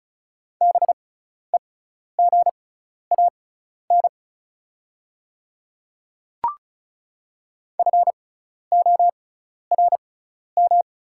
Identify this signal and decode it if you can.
Morse code, heard as a keyed tone, about 36 wpm, spaced as for 12 wpm, 700 Hz. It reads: BEGAN FORM
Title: 6x Character Spacing - Top 300 Words - 35wpm